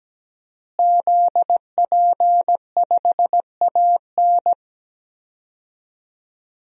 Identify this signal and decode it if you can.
Morse code, heard as a keyed tone, about 17 wpm, 700 Hz. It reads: ZP5AN
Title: DXCC-100 Call Signs - 5 Characters - Repeated 3x - 17wpm